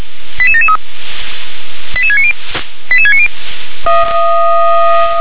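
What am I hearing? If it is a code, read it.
Five-tone selective call (ZVEI), heard as quick runs of five tones, tones of about 70 ms. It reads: 88773 88690 88690
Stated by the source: ZVEI five-tone selective call received with an SDR.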